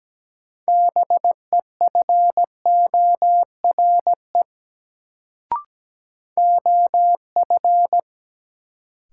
Morse code, heard as a keyed tone, about 17 wpm, 700 Hz. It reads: BEFORE OF
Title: Sets of 2 Words - New Word Drill - 17wpm